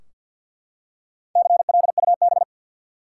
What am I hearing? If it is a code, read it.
Morse code, heard as a keyed tone, about 53 wpm, 700 Hz. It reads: CLUB
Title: Four Letter Words - 50wpm